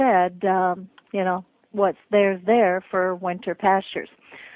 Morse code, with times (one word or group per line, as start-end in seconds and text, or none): none